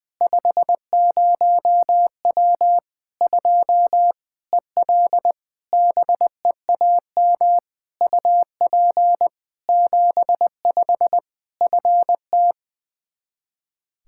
Morse code, heard as a keyed tone, about 20 wpm, 700 Hz. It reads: 50W 2 EL BEAM UP 75 FT
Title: Practice QSO - Exchange 2 - Info - 20wpm